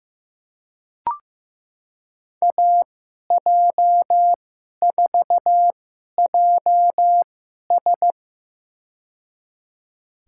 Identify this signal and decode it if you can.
Morse code, heard as a keyed tone, about 15 wpm, spaced as for 11 wpm, 700 Hz. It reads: AJ4JS
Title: USA Call Signs - 5 Characters - 2x Character Spacing - 15wpm